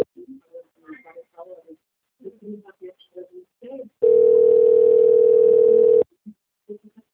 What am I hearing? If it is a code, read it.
Ringback tone (North America), 440 + 480 Hz, with speech.
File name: out- - - - - .32241